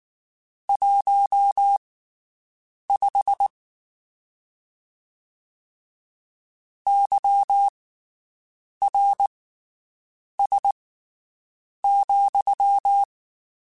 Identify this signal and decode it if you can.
Morse code, heard as a keyed tone, about 19 wpm, spaced as for 7 wpm, 800 Hz. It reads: 15 YRS,
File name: (spaced) / QSOs Spaced-mp3s / WD0BGP 1 @18 sp